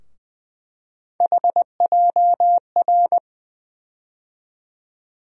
Morse code, heard as a keyed tone, about 20 wpm, 700 Hz. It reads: HJR
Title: USA Call Sign Postfixes - 3 Letters 20wpm